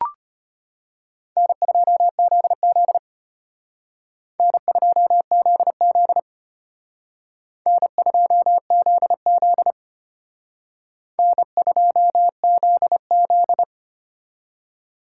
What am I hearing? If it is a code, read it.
Morse code, heard as a keyed tone, about 31 wpm, 700 Hz. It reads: N2ZZ N2ZZ N2ZZ N2ZZ
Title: Speed Racing - USA 1x2 Call Signs 25wpm